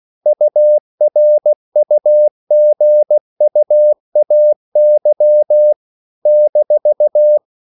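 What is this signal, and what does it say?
Morse code, heard as a keyed tone, about 16 wpm, 600 Hz. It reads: URUGUAY -
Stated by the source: CW practice of national capitals at 16 words per minute, from ON6ZQ.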